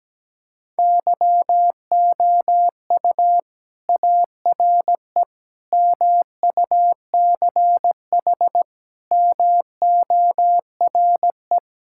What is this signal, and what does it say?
Morse code, heard as a keyed tone, about 17 wpm, 700 Hz. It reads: YOU ARE MUCH MORE
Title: Sentences from Top 1000 Words - Review Part 5 - 17wpm